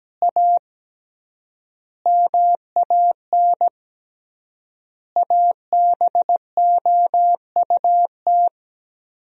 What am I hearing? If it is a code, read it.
Morse code, heard as a keyed tone, about 17 wpm, 700 Hz. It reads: A MAN ABOUT
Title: 3x Word-Spacing - Sentences from Top 400 Words - 17wpm